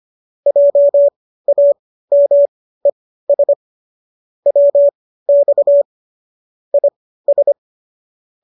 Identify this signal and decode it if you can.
Morse code, heard as a keyed tone, about 25 wpm, spaced as for 15 wpm, 570 Hz. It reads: JAMES WX IS